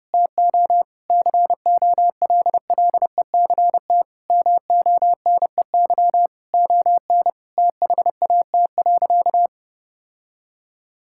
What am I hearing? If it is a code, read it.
Morse code, heard as a keyed tone, about 30 wpm, 700 Hz. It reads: TO COLLECT MONEY ON THAT.
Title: Sentences from Top 1600 Words 30wpm